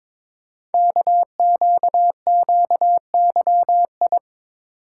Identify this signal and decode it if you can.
Morse code, heard as a keyed tone, about 22 wpm, 700 Hz. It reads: KQQYI
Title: Code Groups - 5 Letter:Numbers - 22wpm